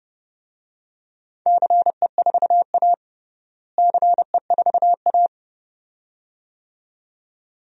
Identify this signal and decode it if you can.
Morse code, heard as a keyed tone, about 30 wpm, 700 Hz. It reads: CE4A CE4A
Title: DXCC-100 Call Signs - 4 Characters - Repeated 2x - 30wpm